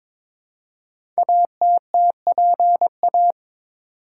Morse code, heard as a keyed tone, about 22 wpm, 700 Hz. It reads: ATTPA
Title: Code Groups - 5 Letters - 22wpm